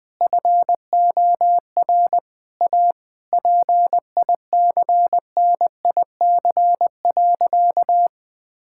Morse code, heard as a keyed tone, about 21 wpm, 700 Hz. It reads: FOR A PICNIC.
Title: Difficult Sentences with Hints - 20wpm